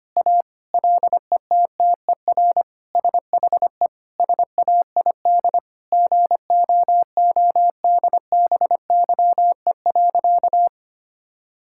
Morse code, heard as a keyed tone, about 25 wpm, 700 Hz. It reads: A LETTER SHE SAID GOODBYE.
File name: practice / 25wpm / Sentences with 4-Word Phrases - 25wpm